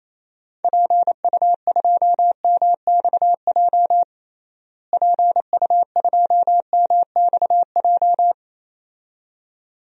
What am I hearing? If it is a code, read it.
Morse code, heard as a keyed tone, about 27 wpm, 700 Hz. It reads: PU2MXJ PU2MXJ
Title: DXCC-100 Call Signs - 6 Characters - Repeated 2x - 28wpm